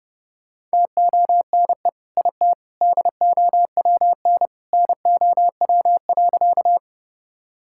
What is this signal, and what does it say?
Morse code, heard as a keyed tone, about 32 wpm, 700 Hz. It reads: TONE IT DOWN NOW.
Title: Sentences from Top 1600 Words 30wpm